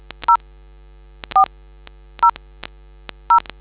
DTMF, heard as telephone touch tones, tones of about 110 ms. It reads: 0 4 0 0